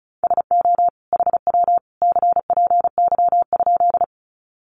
Morse code, heard as a keyed tone, about 34 wpm, 700 Hz. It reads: SO HW CPY?
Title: Common QSO Phrases 35wpm